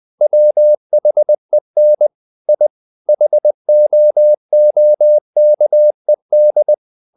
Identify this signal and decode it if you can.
Morse code, heard as a keyed tone, about 20 wpm, 600 Hz. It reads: WHEN I HOOKED
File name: the old man and the sea / the old man and the sea part 1 20wpm